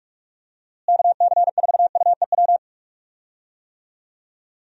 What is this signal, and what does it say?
Morse code, heard as a keyed tone, about 46 wpm, 700 Hz. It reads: KC4UEW
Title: USA Call Signs - 6 Characters - Repeated 2x - 45wpm